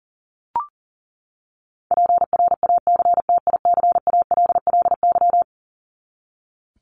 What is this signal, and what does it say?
Morse code, heard as a keyed tone, about 40 wpm, 700 Hz. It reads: PRACTICALLY
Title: LY Suffix - Top 100 Words with Base Words 40wpm